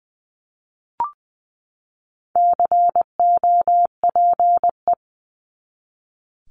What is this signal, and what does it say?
Morse code, heard as a keyed tone, about 20 wpm, 700 Hz. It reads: COPE
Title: 260 Lesson 42e - Words 20-20wpm